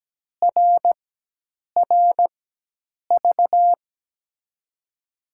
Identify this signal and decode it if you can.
Morse code, heard as a keyed tone, about 17 wpm, spaced as for 8 wpm, 700 Hz. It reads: RRV
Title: USA Call Signs - 6 Characters - 4x Character Spacing - 17wpm